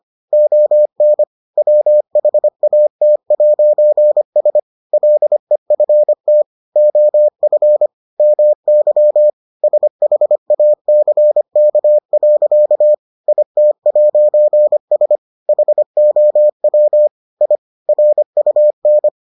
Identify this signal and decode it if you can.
Morse code, heard as a keyed tone, about 25 wpm, 600 Hz. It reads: ON WHAT'S LEFT OF MY SHACK. IT'S HOW I RUN